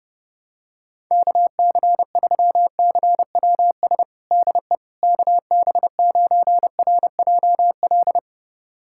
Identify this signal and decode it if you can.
Morse code, heard as a keyed tone, about 30 wpm, 700 Hz. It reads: KC3CWS DE KB9RJL
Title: USA Callsign DE Callsign - 6 Characters - 30wpm